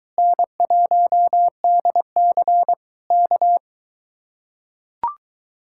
Morse code, heard as a keyed tone, about 23 wpm, 700 Hz. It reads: N1DC K